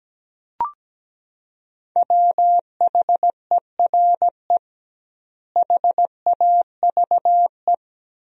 Morse code, heard as a keyed tone, about 17 wpm, 700 Hz. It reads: WHERE HAVE
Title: 2x Word-Spacing - Sentences from Top 200 Words - 17wpm